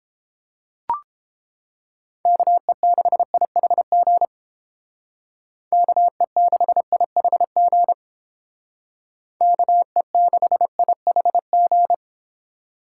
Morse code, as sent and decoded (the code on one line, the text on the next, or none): -.- . -.... .. .... --. / -.- . -.... .. .... --. / -.- . -.... .. .... --.
KE6IHG KE6IHG KE6IHG